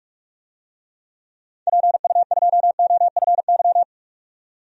Morse code, heard as a keyed tone, about 46 wpm, 700 Hz. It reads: PU2OFY
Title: DXCC-100 Call Signs - 6 Characters - 45wpm